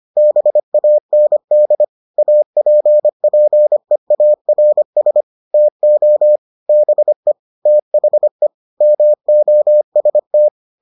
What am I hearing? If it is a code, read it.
Morse code, heard as a keyed tone, about 25 wpm, 600 Hz. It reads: BAND APPEARS TO BE THE MOST